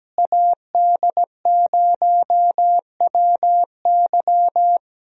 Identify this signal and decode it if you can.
Morse code, heard as a keyed tone, about 17 wpm, 700 Hz. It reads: AD0WY
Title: USA Call Signs - 5 Characters - Repeated 3x - 17wpm